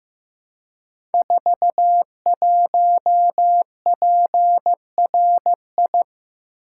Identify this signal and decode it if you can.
Morse code, heard as a keyed tone, about 15 wpm, 700 Hz. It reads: 41PRI
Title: Code Groups - 5 Letter:Numbers - 15wpm